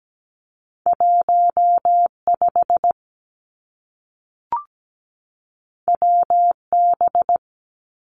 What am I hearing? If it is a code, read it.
Morse code, heard as a keyed tone, about 17 wpm, 700 Hz. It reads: 15 WB
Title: Two Challenging Letter-Number Combinations 17wpm